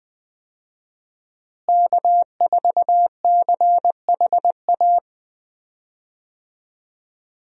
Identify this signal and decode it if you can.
Morse code, heard as a keyed tone, about 20 wpm, 700 Hz. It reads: K4CHA
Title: USA Call Signs - 5 Characters - Repeated 2x - 20wpm